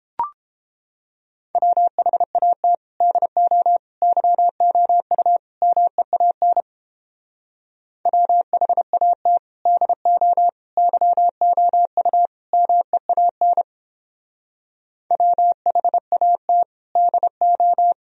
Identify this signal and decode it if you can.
Morse code, heard as a keyed tone, about 29 wpm, 700 Hz. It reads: WHAT DO YOU MEAN WHAT DO YOU MEAN WHAT DO